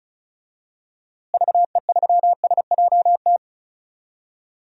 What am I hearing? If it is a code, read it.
Morse code, heard as a keyed tone, about 36 wpm, 700 Hz. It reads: VE3SJT
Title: DXCC-100 Call Signs - 6 Characters - Repeated 3x - 35wpm